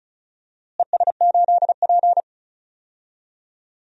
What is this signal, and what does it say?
Morse code, heard as a keyed tone, about 34 wpm, 700 Hz. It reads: ES8P
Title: DXCC-100 Call Signs - 4 Characters - Repeated 2x - 35wpm